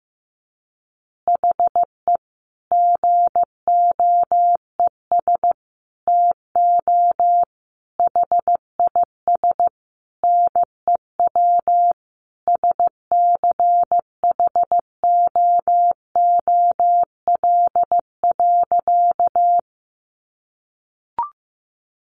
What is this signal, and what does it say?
Morse code, heard as a keyed tone, about 15 wpm, 700 Hz. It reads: HE GOES TO HIS NEW SCHOOL.
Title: Easy Phrases 15wpm